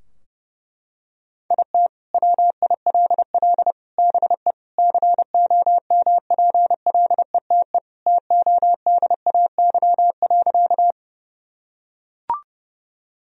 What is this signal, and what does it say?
Morse code, heard as a keyed tone, about 32 wpm, 700 Hz. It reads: IT WILL BE COMPLETE TODAY.
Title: Sentences from Top 1000 Words - Review Part 2 - 30wpm